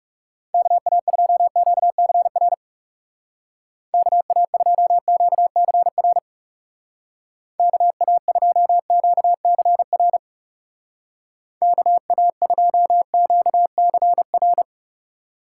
Morse code, heard as a keyed tone, about 36 wpm, 700 Hz. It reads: KA2QCR KA2QCR KA2QCR KA2QCR
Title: Speed Racing - USA 2x3 Call Signs 30wpm